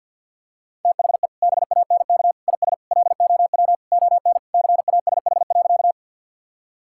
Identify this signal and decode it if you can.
Morse code, heard as a keyed tone, about 50 wpm, 700 Hz. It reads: THE BANK IS LOW ON CASH.
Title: Sentences from Top 1000 Words - Review Part 5 - 50wpm